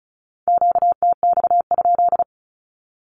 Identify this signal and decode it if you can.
Morse code, heard as a keyed tone, about 36 wpm, 700 Hz. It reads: QTX?
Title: Q-Codes with Spoken Meaning - 35wpm